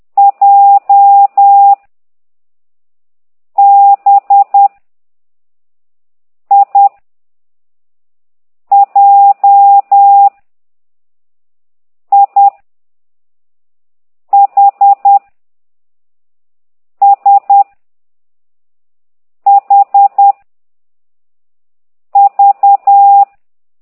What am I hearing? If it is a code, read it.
Morse code, heard as a keyed tone, about 10 wpm, 800 Hz. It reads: J B I J I H S H V
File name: ct2idw / exerc6